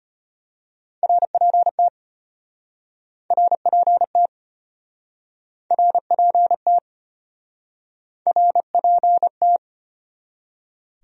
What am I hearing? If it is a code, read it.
Morse code, heard as a keyed tone, about 34 wpm, 700 Hz. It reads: RPT RPT RPT RPT